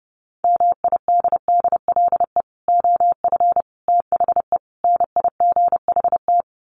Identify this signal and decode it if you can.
Morse code, heard as a keyed tone, about 30 wpm, 700 Hz. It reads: MIDDLE OF THE NIGHT
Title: Sets of 4 Words 30wpm